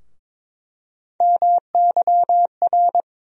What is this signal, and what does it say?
Morse code, heard as a keyed tone, about 22 wpm, 700 Hz. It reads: MYR
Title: USA Call Sign Postfixes - 3 Letters 22wpm